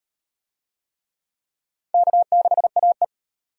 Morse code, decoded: K6AE

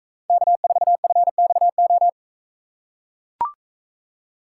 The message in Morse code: K4FXO

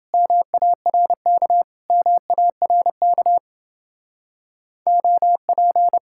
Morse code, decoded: MARK MARK OP

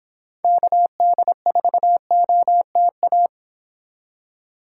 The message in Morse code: KD4OTA

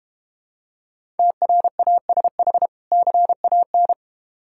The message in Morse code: TRASH CAN

